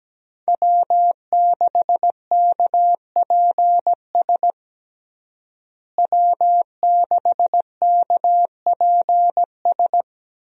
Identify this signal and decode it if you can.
Morse code, heard as a keyed tone, about 17 wpm, 700 Hz. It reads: W6KPS W6KPS
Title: USA Call Signs - 5 Characters - Repeated 2x - 17wpm